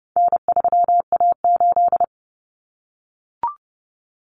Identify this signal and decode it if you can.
Morse code, heard as a keyed tone, about 30 wpm, 700 Hz. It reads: N3A8
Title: Common Cut Numbers - 4-Digit 30wpm